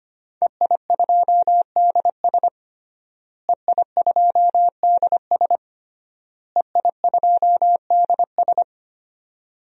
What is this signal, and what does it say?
Morse code, heard as a keyed tone, about 25 wpm, 700 Hz. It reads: EI2DS EI2DS EI2DS
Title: DXCC-100 Call Signs - 5 Characters - Repeated 3x - 25wpm